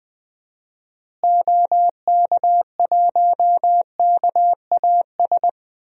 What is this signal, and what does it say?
Morse code, heard as a keyed tone, about 21 wpm, 700 Hz. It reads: OK1KAS
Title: DXCC-100 Call Signs - 6 Characters - Repeated 3x - 20wpm